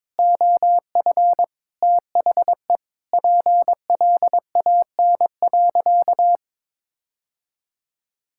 Morse code, heard as a keyed tone, about 22 wpm, 700 Hz. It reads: OF THE PLAN.
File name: practice / 22wpm / Sentences with 4-Word Phrases - 22wpm